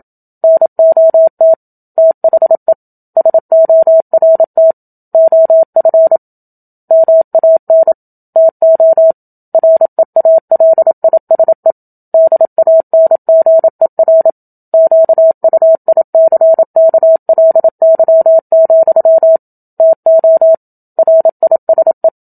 Morse code, heard as a keyed tone, about 27 wpm, 650 Hz. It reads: NOT THE SORT OF MAN TO REALISE DANGER QUICKLY, TO RISE
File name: Book Two / 01-Chapter 1 - 27 WPM